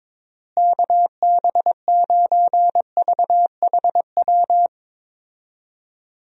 Morse code, heard as a keyed tone, about 22 wpm, 700 Hz. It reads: KB9VHW
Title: USA Call Signs - 6 Characters - Repeated 2x - 22wpm